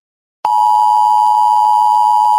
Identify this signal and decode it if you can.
Emergency Alert System (SAME) tones (the 853 and 960 Hz attention tone). It is heard from 0.4 to 2.4 s.
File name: cloud download